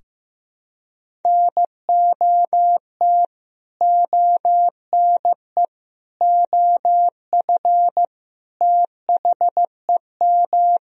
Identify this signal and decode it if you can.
Morse code, heard as a keyed tone, about 15 wpm, 700 Hz. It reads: NOT ONE OF THEM
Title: Sentences from Top 1000 Words - Review Part 4 - 15wpm